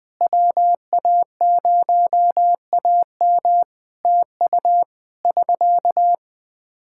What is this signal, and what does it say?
Morse code, heard as a keyed tone, about 20 wpm, 700 Hz. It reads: WA0AM TU <SK>
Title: Practice QSO - Exchange 4 - Ending - 20wpm